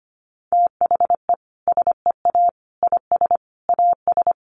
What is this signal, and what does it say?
Morse code, heard as a keyed tone, about 25 wpm, 700 Hz. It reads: THE SEA IS AS
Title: Sentences from Top 1400 Words 25wpm